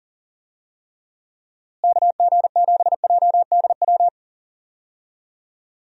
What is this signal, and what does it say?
Morse code, heard as a keyed tone, about 40 wpm, 700 Hz. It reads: KG7JDW
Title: USA Call Signs - 6 Characters - Repeated 2x - 40wpm